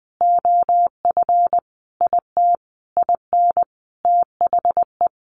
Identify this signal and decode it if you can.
Morse code, heard as a keyed tone, about 20 wpm, 700 Hz. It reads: OF IT IN THE
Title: Sentences from Top 900 Words 20wpm